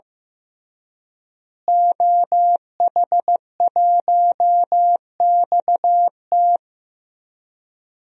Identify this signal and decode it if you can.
Morse code, heard as a keyed tone, about 16 wpm, 700 Hz. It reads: OH1XT
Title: DXCC-100 Call Signs - 5 Characters - Repeated 3x - 15wpm